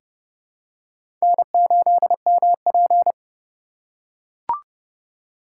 Morse code, heard as a keyed tone, about 30 wpm, 700 Hz. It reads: N8MP
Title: USA Call Signs - 4 Characters - 3x Character Spacing - 30wpm